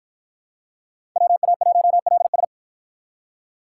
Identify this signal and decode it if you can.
Morse code, heard as a keyed tone, about 53 wpm, 700 Hz. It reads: WA1LS